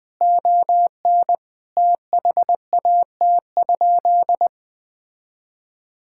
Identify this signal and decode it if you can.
Morse code, heard as a keyed tone, about 20 wpm, 700 Hz. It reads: ON THAT?
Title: Sentences from Top 1600 Words 20wpm